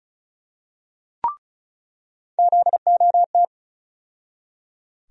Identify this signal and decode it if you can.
Morse code, heard as a keyed tone, about 36 wpm, 700 Hz. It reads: ZOT
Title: USA Call Sign Postfixes - 3 Letters 35wpm